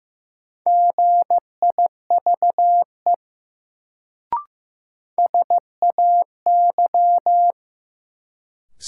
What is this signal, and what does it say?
Morse code, heard as a keyed tone, about 15 wpm, 700 Hz. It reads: GIVE SAY
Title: Sentences from Top 200 Words - New Word Drill - 15wpm